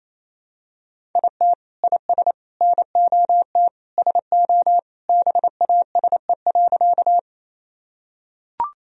Morse code, heard as a keyed tone, about 28 wpm, 700 Hz. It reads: IT IS NOT SO BASE.
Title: Sentences from Top 1000 Words - Review Part 1 - 28wpm